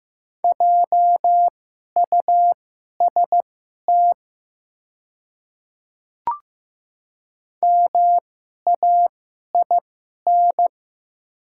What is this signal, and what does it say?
Morse code, heard as a keyed tone, about 15 wpm, spaced as for 11 wpm, 700 Hz. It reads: JUST MAIN
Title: 2x Character Spacing - Top 300 Words - 15wpm